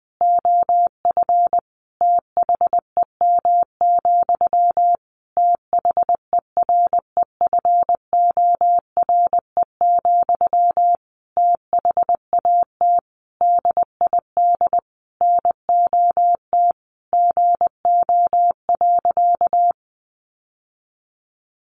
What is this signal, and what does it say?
Morse code, heard as a keyed tone, about 20 wpm, 700 Hz. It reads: OF THEM, THEREFORE, THAT DID NOT GO.
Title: Sentences from Top 800 Words 20wpm